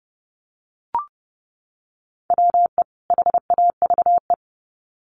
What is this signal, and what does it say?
Morse code, heard as a keyed tone, about 30 wpm, 700 Hz. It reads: WE HAVE
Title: Sets of 2 Words 30wpm